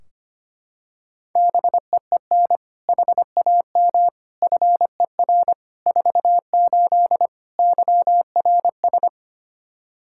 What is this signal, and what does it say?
Morse code, heard as a keyed tone, about 25 wpm, 700 Hz. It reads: BEEN HAM FER 48 YRS
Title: Practice QSO - Exchange 3 - More Info - 25wpm